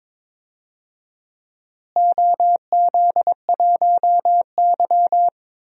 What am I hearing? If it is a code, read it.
Morse code, heard as a keyed tone, about 22 wpm, 700 Hz. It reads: OZ1Y
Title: DXCC-100 Call Signs - 4 Characters - Repeated 2x - 22wpm